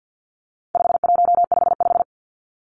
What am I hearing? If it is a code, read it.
Morse code, heard as a keyed tone, about 50 wpm, 700 Hz. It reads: 5155
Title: Numbers - 4-Digit 50wpm